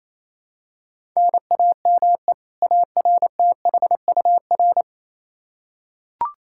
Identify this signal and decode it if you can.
Morse code, heard as a keyed tone, about 28 wpm, 700 Hz. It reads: NAME ARTHUR
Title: Practice QSO - Exchange 1 - Beginning - 28wpm